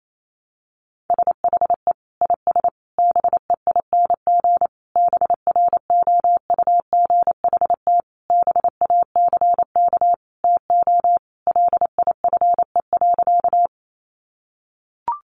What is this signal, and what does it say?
Morse code, heard as a keyed tone, about 28 wpm, 700 Hz. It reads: SHE IS BEING BROUGHT BACK TO LIFE.
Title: Sentences from Top 600 Words 28wpm